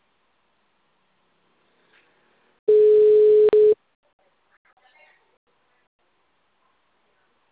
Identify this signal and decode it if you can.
Ringback tone (Europe, 425 Hz).
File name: out- -2005- - - .16065